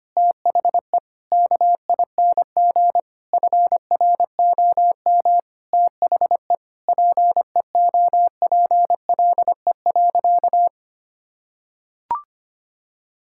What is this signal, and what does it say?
Morse code, heard as a keyed tone, about 25 wpm, 700 Hz. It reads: THE KING FROM THE PEOPLE.
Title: Sentences from Top 1000 Words - Review Part 6 - 25wpm